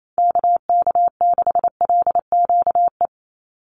Morse code, heard as a keyed tone, about 28 wpm, 700 Hz. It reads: KK6LQE
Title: USA Call Signs 28wpm